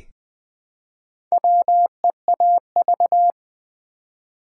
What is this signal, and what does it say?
Morse code, heard as a keyed tone, about 20 wpm, 700 Hz. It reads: WEAV
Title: Build A Word - 1 Letter At A Time - 3 to 5-Letter Words - 20wpm